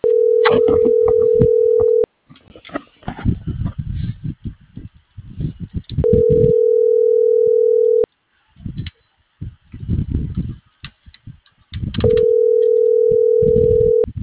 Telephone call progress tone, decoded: ringback tone

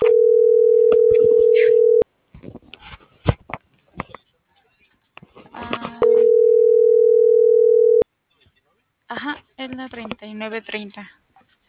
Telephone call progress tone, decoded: ringback tone

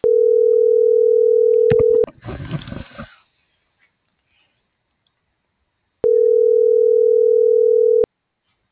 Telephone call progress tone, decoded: ringback tone

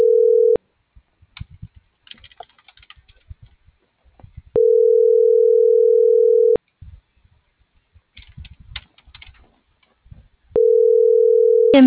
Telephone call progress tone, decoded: ringback tone

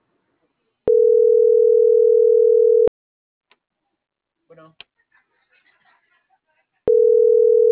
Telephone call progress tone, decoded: ringback tone

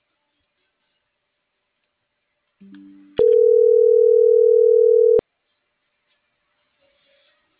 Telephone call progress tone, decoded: ringback tone

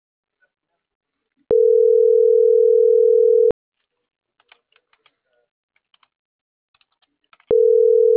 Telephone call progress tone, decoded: ringback tone

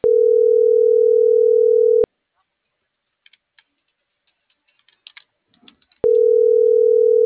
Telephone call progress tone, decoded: ringback tone

